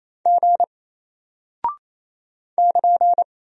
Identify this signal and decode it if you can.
Morse code, heard as a keyed tone, about 28 wpm, 700 Hz. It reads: G <KN>
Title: Single Character (Letter:Number:Punctuation:Prosign) Alt 28wpm